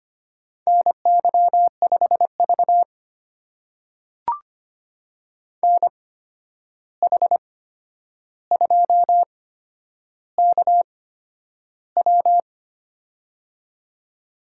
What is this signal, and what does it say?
Morse code, heard as a keyed tone, about 25 wpm, spaced as for 10 wpm, 700 Hz. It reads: NY5V NH2KW
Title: USA Call Signs - 8x Character Spacing - 25wpm